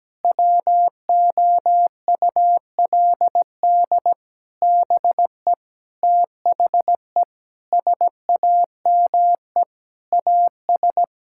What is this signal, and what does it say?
Morse code, heard as a keyed tone, about 17 wpm, 700 Hz. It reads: WOULD BE THE SAME AS
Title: Sentences from Top 1000 Words - Review Part 5 - 17wpm